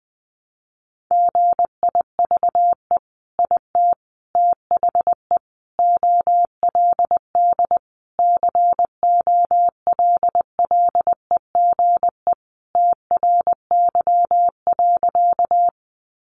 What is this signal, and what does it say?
Morse code, heard as a keyed tone, about 19 wpm, 700 Hz. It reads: GIVE IT THE OLD COLLEGE TRY.
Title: Sentences from Top 1300 Words 20wpm